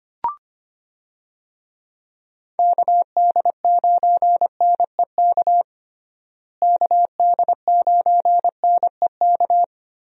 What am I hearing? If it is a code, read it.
Morse code, heard as a keyed tone, about 25 wpm, 700 Hz. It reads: KD9NEK KD9NEK
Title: USA Call Signs - 6 Characters - Repeated 3x - 25wpm